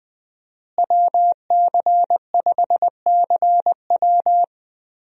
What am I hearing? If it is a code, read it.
Morse code, heard as a keyed tone, about 21 wpm, 700 Hz. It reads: WC5CW
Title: USA Call Signs - 5 Characters - Repeated 3x - 20wpm